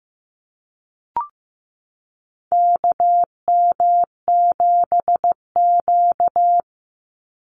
Morse code, heard as a keyed tone, about 15 wpm, 700 Hz. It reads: KM7Q
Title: Call Signs Repeated 2x - 15wpm